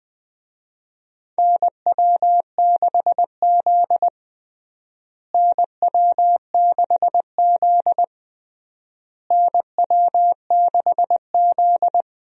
Morse code, heard as a keyed tone, about 20 wpm, 700 Hz. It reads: NW6Z NW6Z NW6Z
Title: USA Call Signs - 4 Characters - Repeated 3x - 20wpm